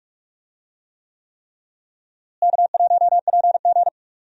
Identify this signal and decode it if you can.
Morse code, heard as a keyed tone, about 46 wpm, 700 Hz. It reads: K1PG
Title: USA Call Signs - 4 Characters - Repeated 3x - 45wpm